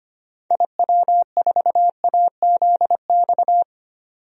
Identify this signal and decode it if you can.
Morse code, heard as a keyed tone, about 25 wpm, 700 Hz. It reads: IW4AZX
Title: DXCC-100 Call Signs - 6 Characters - 25wpm